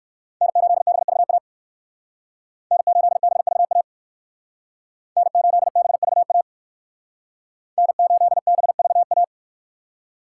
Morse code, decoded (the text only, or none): N8BVA N8BVA N8BVA N8BVA